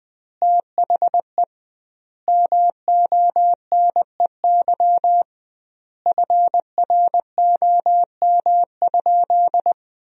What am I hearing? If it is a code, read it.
Morse code, heard as a keyed tone, about 20 wpm, 700 Hz. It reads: THE MONEY FROM?
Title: 2x Word-Spacing - Sentences from Top 400 Words - 20wpm